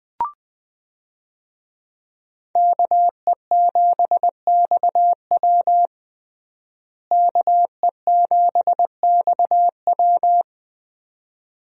Morse code, heard as a keyed tone, about 20 wpm, 700 Hz. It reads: KE7XW KE7XW
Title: USA Call Signs - 5 Characters - Repeated 2x - 20wpm